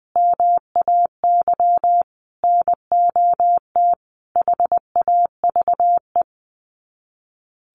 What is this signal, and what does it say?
Morse code, heard as a keyed tone, about 20 wpm, 700 Hz. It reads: MAY NOT HAVE
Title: Sets of 5 Words Encore2 20wpm